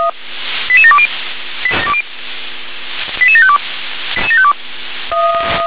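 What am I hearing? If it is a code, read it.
Five-tone selective call (ZVEI), heard as quick runs of five tones, tones of about 70 ms. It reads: 88720 88752
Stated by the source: Five-tone ZVEI selective call 88757, received by SDR.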